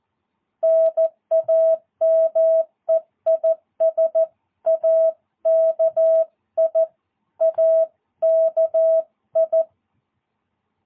Morse code, heard as a keyed tone, about 14 wpm, 660 Hz. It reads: NAME IS AKI AKI
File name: jh8cbh / 2022cw